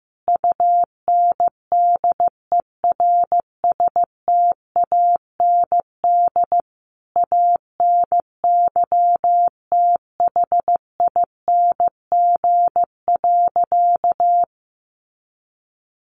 Morse code, heard as a keyed tone, about 15 wpm, 700 Hz. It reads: UNDERSTAND ANYTHING.
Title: Sentences from Top 1000 Words 15wpm